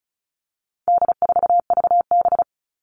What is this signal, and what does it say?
Morse code, heard as a keyed tone, about 34 wpm, 700 Hz. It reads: D4VB
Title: All Cut Numbers - 4-Digit 35wpm